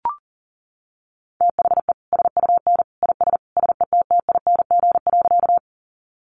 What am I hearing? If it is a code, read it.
Morse code, heard as a keyed tone, about 40 wpm, 700 Hz. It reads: THE SUN IS SETTING.
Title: Sentences from Top 1300 Words 40wpm